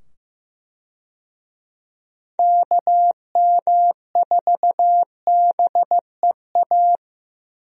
Morse code, heard as a keyed tone, about 15 wpm, 700 Hz. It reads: KM4BEA
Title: USA Call Signs - 6 Characters - 15wpm